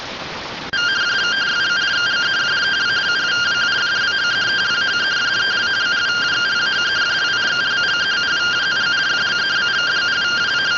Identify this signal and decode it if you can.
Two-tone teleprinter signal, about 170 Hz shift.